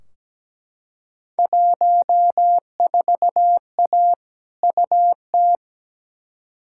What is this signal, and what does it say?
Morse code, heard as a keyed tone, about 17 wpm, 700 Hz. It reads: 14A UT
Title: Field Day Exchanges Encore - 17wpm